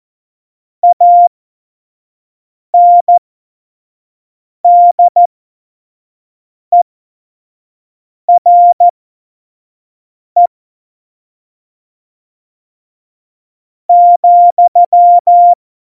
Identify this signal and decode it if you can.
Morse code, heard as a keyed tone, about 14 wpm, spaced as for 4 wpm, 700 Hz. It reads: ANDERE,
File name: Morsekurs / Witze / Witz 17.Juli 2018, 25 F70